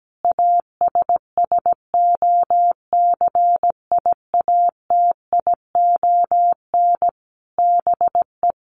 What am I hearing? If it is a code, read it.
Morse code, heard as a keyed tone, about 17 wpm, 700 Hz. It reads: ASSOCIATION BE
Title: Sentences from Top 1300 Words 17wpm